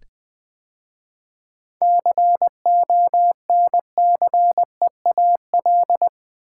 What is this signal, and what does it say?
Morse code, heard as a keyed tone, about 20 wpm, 700 Hz. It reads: CONCEAL
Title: 7 Letter Words - First 3 Letters Given - 20wpm